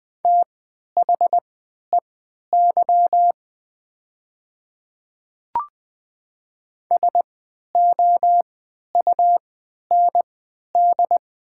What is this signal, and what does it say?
Morse code, heard as a keyed tone, about 20 wpm, spaced as for 11 wpm, 700 Hz. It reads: THEY SOUND
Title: 3x Character Spacing - Top 100 Words - 20wpm